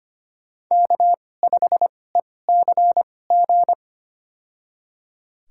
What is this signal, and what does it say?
Morse code, heard as a keyed tone, about 25 wpm, spaced as for 18 wpm, 700 Hz. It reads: K5ECG
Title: USA Call Signs - 2x Character Spacing - 25wpm